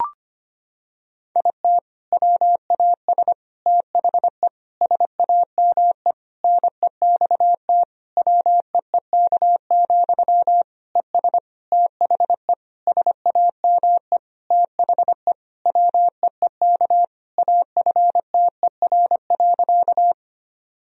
Morse code, heard as a keyed tone, about 25 wpm, 700 Hz. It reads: IT WAS THE SAME NEXT WEEK, ES THE SAME THE WEEK AFTER.
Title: Sentences from Top 500 Words and Abbreviations 25wpm